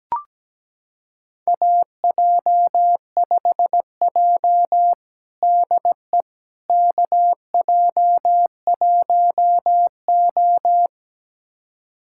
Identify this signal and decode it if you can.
Morse code, heard as a keyed tone, about 17 wpm, 700 Hz. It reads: AJ5J DE KJ1O
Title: USA Callsign DE Callsign - 4 Characters - 17wpm